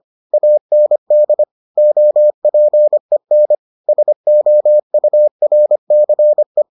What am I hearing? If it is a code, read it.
Morse code, heard as a keyed tone, about 25 wpm, 600 Hz. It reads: AND OPEN SOURCE